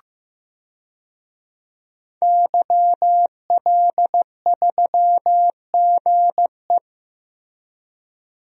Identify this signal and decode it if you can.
Morse code, heard as a keyed tone, about 15 wpm, 700 Hz. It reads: YL3GE
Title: DXCC-100 Call Signs - 5 Characters - Repeated 3x - 15wpm